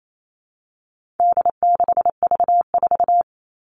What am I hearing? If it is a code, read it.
Morse code, heard as a keyed tone, about 28 wpm, 700 Hz. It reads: D6V4
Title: All Cut Numbers - 4-Digit 28wpm